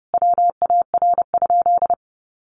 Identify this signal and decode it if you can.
Morse code, heard as a keyed tone, about 30 wpm, 700 Hz. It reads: WAR?
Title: Sentences from Top 1200 Words 30wpm